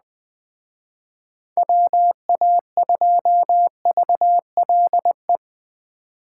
Morse code, heard as a keyed tone, about 20 wpm, 700 Hz. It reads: WA2VLE
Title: Speed Racing - USA 2x3 Call Signs 17wpm